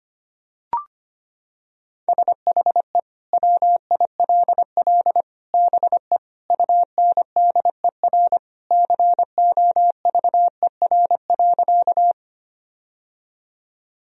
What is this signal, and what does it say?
Morse code, heard as a keyed tone, about 25 wpm, 700 Hz. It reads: SHE WILL BE UNDER COVER.
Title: Sentences from Top 1000 Words - Review Part 1 - 25wpm